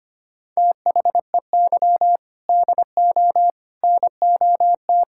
Morse code, heard as a keyed tone, about 25 wpm, 700 Hz. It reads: THEY DO NOT
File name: practice / 25wpm / Sentences with 3-Word Phrases - 25wpm